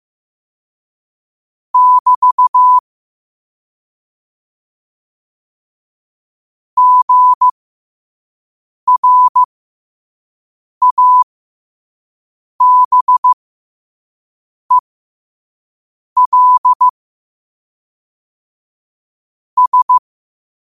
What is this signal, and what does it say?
Morse code, heard as a keyed tone, about 15 wpm, spaced as for 5 wpm, 1000 Hz. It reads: <BT> GRABEL S
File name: qotdfiles / 5wpm